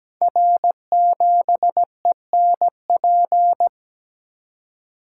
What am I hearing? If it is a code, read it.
Morse code, heard as a keyed tone, about 17 wpm, 700 Hz. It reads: R7ENP
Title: Code Groups - 5 Letter:Numbers - 17wpm